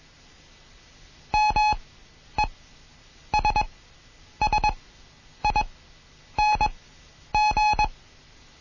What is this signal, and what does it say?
Morse code, heard as a keyed tone, about 21 wpm, spaced as for 9 wpm, 850 Hz. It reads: MESSING
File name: Morsen / Kurs A 20